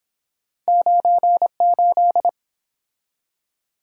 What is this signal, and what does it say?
Morse code, heard as a keyed tone, about 30 wpm, 700 Hz. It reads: 98